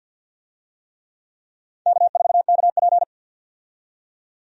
Morse code, heard as a keyed tone, about 52 wpm, 700 Hz. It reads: K4KP
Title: USA Call Signs - 4 Characters - 50wpm